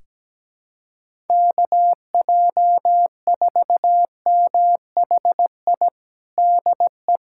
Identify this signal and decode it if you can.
Morse code, heard as a keyed tone, about 17 wpm, 700 Hz. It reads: KJ4MHI DE
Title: USA Callsign DE Callsign - 6 Characters - 17wpm